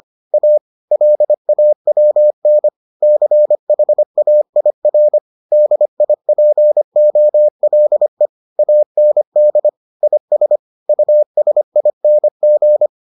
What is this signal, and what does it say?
Morse code, heard as a keyed tone, about 25 wpm, 600 Hz. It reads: A LAWN CHAIR DIPOLE AND IS USING